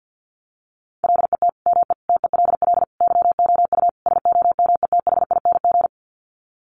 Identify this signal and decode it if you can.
Morse code, heard as a keyed tone, about 50 wpm, 700 Hz. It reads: LET ME TELL YOU SOMETHING
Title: Sets of 5 Words 50wpm